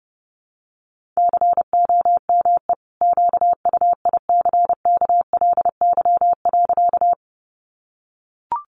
Morse code, heard as a keyed tone, about 30 wpm, 700 Hz. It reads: COME QUICKLY.
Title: Sentences from Top 700 Words 30wpm